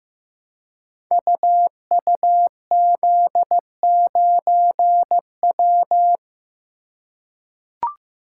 Morse code, heard as a keyed tone, about 15 wpm, 700 Hz. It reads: UUZ9W
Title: Code Groups - 5 Letter:Numbers - 15wpm